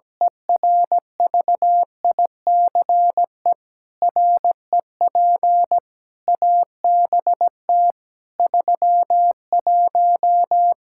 Morse code, heard as a keyed tone, about 17 wpm, 700 Hz. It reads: ERVICE REP ABT 31
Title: Practice QSO - Exchange 3 - More Info - 17wpm